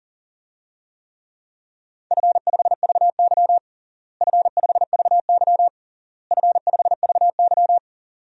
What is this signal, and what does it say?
Morse code, heard as a keyed tone, about 40 wpm, 700 Hz. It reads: F5VY F5VY F5VY
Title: DXCC-100 Call Signs - 4 Characters - Repeated 3x - 40wpm